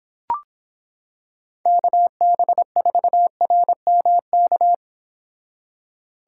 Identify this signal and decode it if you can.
Morse code, heard as a keyed tone, about 26 wpm, 700 Hz. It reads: KB4RMK